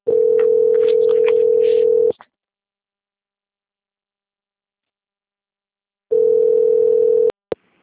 Ringback tone (North America), 440 + 480 Hz.